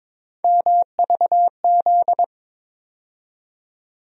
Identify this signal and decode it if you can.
Morse code, heard as a keyed tone, about 22 wpm, 700 Hz. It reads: MVZ
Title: USA Call Sign Postfixes - 3 Letters 22wpm